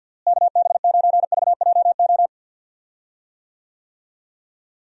Morse code, heard as a keyed tone, about 50 wpm, 700 Hz. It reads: KB9VJO
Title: USA Call Signs - 6 Characters - Repeated 3x - 50wpm